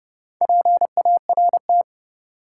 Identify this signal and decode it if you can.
Morse code, heard as a keyed tone, about 30 wpm, 700 Hz. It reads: PART
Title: Sentences from Top 200 Words - New Word Drill - 30wpm